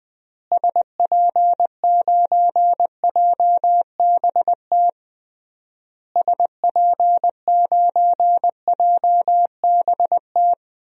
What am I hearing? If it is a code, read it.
Morse code, heard as a keyed tone, about 20 wpm, 700 Hz. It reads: SP9JBT SP9JBT
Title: DXCC-100 Call Signs - 6 Characters - Repeated 3x - 20wpm